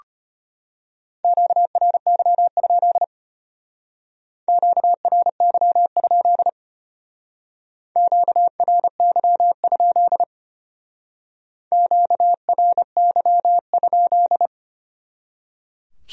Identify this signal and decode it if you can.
Morse code, heard as a keyed tone, about 32 wpm, 700 Hz. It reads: QRY? QRY? QRY? QRY?